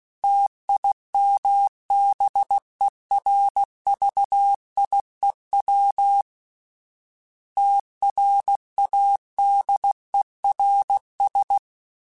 Morse code, heard as a keyed tone, about 16 wpm, 800 Hz. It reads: TIMBERVIEW TRADERS
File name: fromSMs / FromSMs mp3s / Cities@15